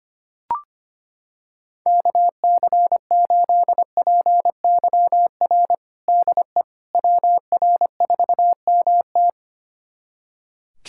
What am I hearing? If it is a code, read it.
Morse code, heard as a keyed tone, about 25 wpm, 700 Hz. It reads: KC8PYR DE WR4MT
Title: USA Callsign DE Callsign 25wpm